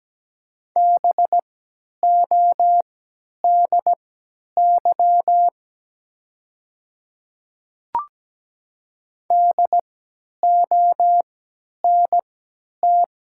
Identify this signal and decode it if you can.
Morse code, heard as a keyed tone, about 17 wpm, spaced as for 10 wpm, 700 Hz. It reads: BODY DONT
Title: 3x Character Spacing - Top 500 Words - 17wpm